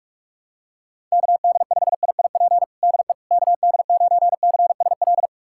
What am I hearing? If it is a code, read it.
Morse code, heard as a keyed tone, about 46 wpm, 700 Hz. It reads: KD5IIP DE KD9CSL
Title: USA Callsign DE Callsign - 6 Characters - 45wpm